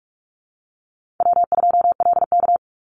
Morse code, heard as a keyed tone, about 46 wpm, 700 Hz. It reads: W2LK
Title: Easy Call Signs - 45wpm